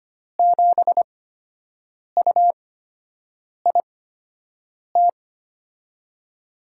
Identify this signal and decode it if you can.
Morse code, heard as a keyed tone, about 25 wpm, spaced as for 7 wpm, 700 Hz. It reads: 7UIT
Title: USA Call Signs - 6 Characters - 8x Character Spacing - 25wpm